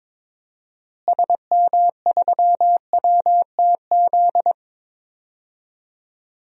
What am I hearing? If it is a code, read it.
Morse code, heard as a keyed tone, about 22 wpm, 700 Hz. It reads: SM3WTZ
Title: DXCC-100 Call Signs - 6 Characters - Repeated 2x - 22wpm